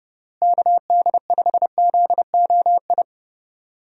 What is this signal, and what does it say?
Morse code, heard as a keyed tone, about 31 wpm, 700 Hz. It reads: KD5ZOI
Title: USA Call Signs - 6 Characters - Repeated 3x - 30wpm